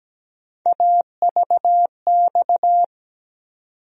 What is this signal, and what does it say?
Morse code, heard as a keyed tone, about 17 wpm, 700 Hz. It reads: AVX
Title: USA Call Sign Postfixes - 3 Letters 17wpm